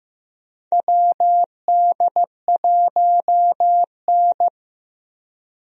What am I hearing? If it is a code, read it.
Morse code, heard as a keyed tone, about 15 wpm, 700 Hz. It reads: WD1N
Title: USA Call Signs - 4 Characters - 15wpm